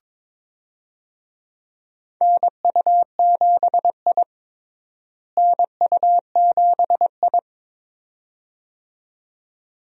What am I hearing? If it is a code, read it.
Morse code, heard as a keyed tone, about 21 wpm, 700 Hz. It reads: NU7I NU7I
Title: USA Call Signs - 4 Characters - Repeated 2x - 22wpm